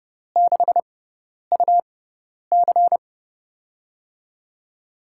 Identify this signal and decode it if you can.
Morse code, heard as a keyed tone, about 29 wpm, spaced as for 10 wpm, 700 Hz. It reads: 6UC
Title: USA Call Signs - 6x Character Spacing - 30wpm